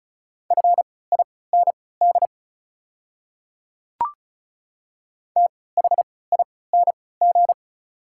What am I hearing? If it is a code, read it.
Morse code, heard as a keyed tone, about 34 wpm, spaced as for 20 wpm, 690 Hz. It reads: FIND THING
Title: 3x Character Spacing - Top 100 Words - 35wpm